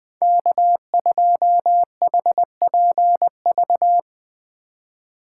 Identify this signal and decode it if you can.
Morse code, heard as a keyed tone, about 23 wpm, 700 Hz. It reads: K2HPV
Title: Speed Racing - USA 1x3 Call Signs 20wpm